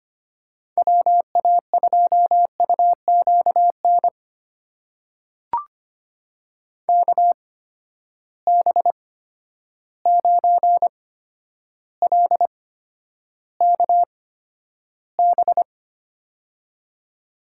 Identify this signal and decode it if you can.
Morse code, heard as a keyed tone, about 25 wpm, spaced as for 11 wpm, 700 Hz. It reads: WA2UQN KB9LKB
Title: USA Call Signs - 6 Characters - 8x Character Spacing - 25wpm